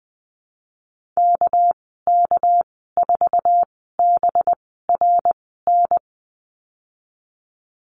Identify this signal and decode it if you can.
Morse code, heard as a keyed tone, about 20 wpm, spaced as for 15 wpm, 700 Hz. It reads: KK4BRN
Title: 213 Lesson 34g - Number 8 - Call Signs 20wpm